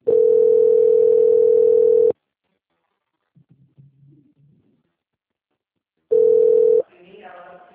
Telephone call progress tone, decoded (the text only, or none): ringback tone